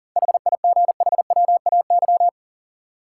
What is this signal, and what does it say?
Morse code, heard as a keyed tone, about 40 wpm, 700 Hz. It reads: HIGHWAY
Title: Seven Letter Words - 40wpm